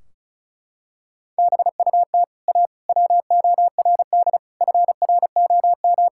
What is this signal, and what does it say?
Morse code, heard as a keyed tone, about 34 wpm, 700 Hz. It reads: BUT A WORD FROM
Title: Sentences from Top 1600 Words 35wpm